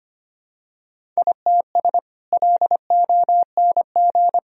Morse code, heard as a keyed tone, about 25 wpm, 700 Hz. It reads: ITS LONG
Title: Sentences from Top 1000 Words - Review Part 1 - 25wpm